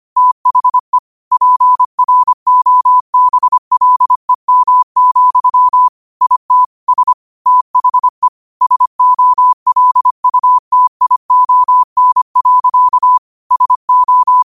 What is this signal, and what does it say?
Morse code, heard as a keyed tone, about 25 wpm, 1000 Hz. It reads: THE PROBLEM, IT S THE SOLUTION. SO